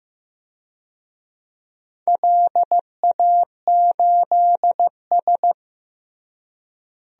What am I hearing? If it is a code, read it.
Morse code, heard as a keyed tone, about 15 wpm, 700 Hz. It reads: LA8S
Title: DXCC-100 Call Signs - 4 Characters - Repeated 2x - 15wpm